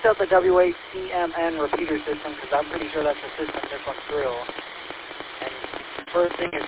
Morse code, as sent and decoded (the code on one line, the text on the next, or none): none